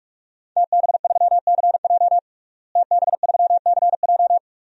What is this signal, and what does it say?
Morse code, heard as a keyed tone, about 46 wpm, 700 Hz. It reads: TB3CJ TB3CJ
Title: DXCC-100 Call Signs - 5 Characters - Repeated 2x - 45wpm